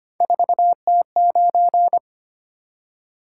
Morse code, heard as a keyed tone, about 25 wpm, 700 Hz. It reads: 4T9
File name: Three Letter-Number Combinations - 25wpm